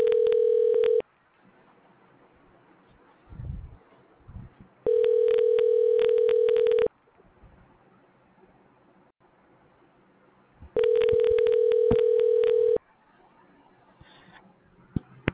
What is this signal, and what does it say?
Ringback tone (North America), 440 + 480 Hz.